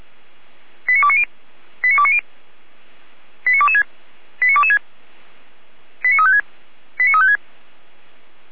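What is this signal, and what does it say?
Five-tone selective call (ZVEI), heard as quick runs of five tones, tones of about 70 ms. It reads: 89290 89290 89226 89226 89356 89356
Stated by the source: ZVEI five-tone selective call received on an SDR.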